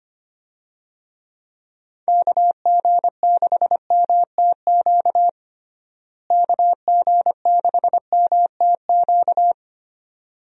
Morse code, heard as a keyed tone, about 25 wpm, 700 Hz. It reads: KG6MTQ KG6MTQ
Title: USA Call Signs - 6 Characters - Repeated 3x - 25wpm